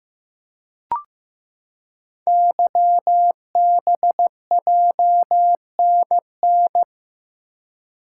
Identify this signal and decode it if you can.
Morse code, heard as a keyed tone, about 15 wpm, 700 Hz. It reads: YBJNN
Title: Code Groups - 5 Letters - 15wpm